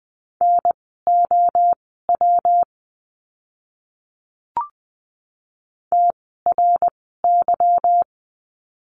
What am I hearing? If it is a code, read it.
Morse code, heard as a keyed tone, about 20 wpm, spaced as for 15 wpm, 700 Hz. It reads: NOW TRY
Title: 230 Lesson 37f - Letter x - Words 20wpm